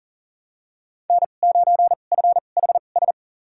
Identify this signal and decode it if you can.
Morse code, heard as a keyed tone, about 40 wpm, spaced as for 29 wpm, 700 Hz. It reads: N9FHS
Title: USA Call Signs - 5 Characters - 2x Character Spacing - 40wpm